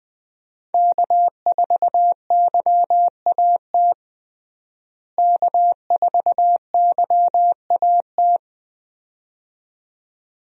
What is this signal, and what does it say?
Morse code, heard as a keyed tone, about 20 wpm, 700 Hz. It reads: K4YAT K4YAT
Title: USA Call Signs - 5 Characters - Repeated 2x - 20wpm